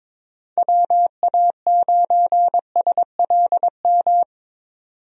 Morse code, heard as a keyed tone, about 22 wpm, 700 Hz. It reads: WA9SLM
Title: USA Call Signs - 6 Characters - 22wpm